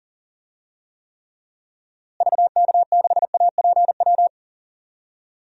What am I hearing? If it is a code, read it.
Morse code, heard as a keyed tone, about 37 wpm, 700 Hz. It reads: VK6APW
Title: DXCC-100 Call Signs - 6 Characters - Repeated 3x - 40wpm